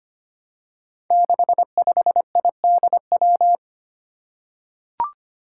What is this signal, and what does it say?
Morse code, heard as a keyed tone, about 25 wpm, 700 Hz. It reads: 65IDW
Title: Five Challenging Letter-Number Combinations 25wpm